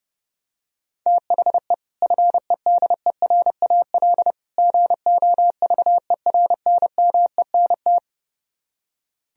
Morse code, heard as a keyed tone, about 30 wpm, 700 Hz. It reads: THE FEDERAL GOVERNMENT